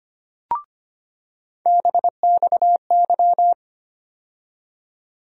Morse code, heard as a keyed tone, about 25 wpm, 700 Hz. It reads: BXY
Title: USA Call Sign Postfixes - 3 Letters 25wpm